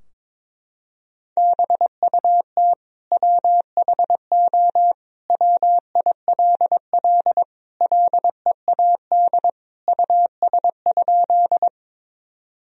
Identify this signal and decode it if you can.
Morse code, heard as a keyed tone, about 22 wpm, 700 Hz. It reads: BUT WHO WILL LEAD US?
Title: Sentences from Top 1000 Words - Review Part 2 - 22wpm